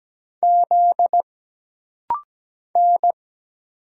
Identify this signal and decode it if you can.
Morse code, heard as a keyed tone, about 17 wpm, 700 Hz. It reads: Z N
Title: Single Letter-Number Alt - Rapid-Fire 17wpm